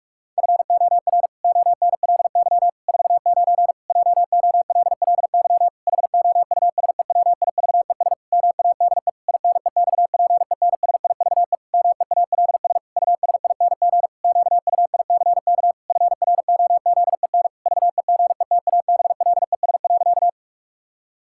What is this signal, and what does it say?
Morse code, heard as a keyed tone, about 41 wpm, 700 Hz. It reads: FOR ONLY 49 JOLLY HOUSEWIVES MADE INEXPENSIVE MEALS USING QUICK FROZEN VEGETABLES.
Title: Pangram Challenge v2 - 45wpm